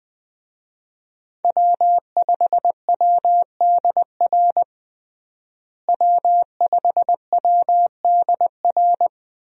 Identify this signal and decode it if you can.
Morse code, heard as a keyed tone, about 20 wpm, 700 Hz. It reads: W5WDR W5WDR
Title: USA Call Signs - 5 Characters - Repeated 2x - 20wpm